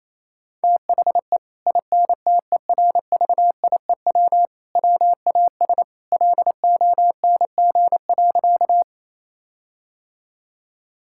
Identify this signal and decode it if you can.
Morse code, heard as a keyed tone, about 28 wpm, 700 Hz. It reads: THE INTERVIEW WAS LONG.
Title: Sentences from Top 1600 Words 28wpm